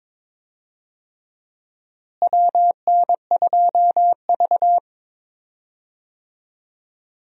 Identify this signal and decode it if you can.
Morse code, heard as a keyed tone, about 22 wpm, 700 Hz. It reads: WN2V
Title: USA Call Signs - 4 Characters - Repeated 3x - 22wpm